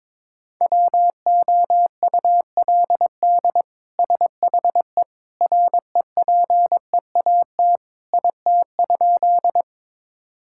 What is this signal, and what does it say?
Morse code, heard as a keyed tone, about 22 wpm, 700 Hz. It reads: WOULD SHE REPEAT IT?
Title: Sentences from Top 1600 Words 22wpm